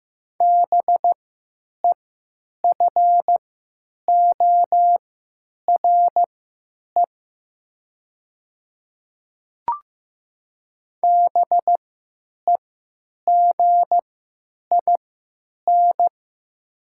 Morse code, heard as a keyed tone, about 15 wpm, spaced as for 9 wpm, 700 Hz. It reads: BEFORE BEGIN
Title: 3x Character Spacing - Top 300 Words - 15wpm